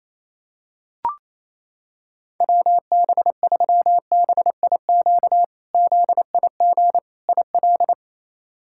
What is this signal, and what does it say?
Morse code, heard as a keyed tone, about 28 wpm, 700 Hz. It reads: WB3BIQ ZIG IL
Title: K1USN SST QSO Practice - 28wpm